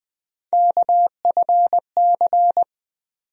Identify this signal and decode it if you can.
Morse code, heard as a keyed tone, about 20 wpm, 700 Hz. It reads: KFC
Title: USA Call Sign Postfixes - 3 Letters 20wpm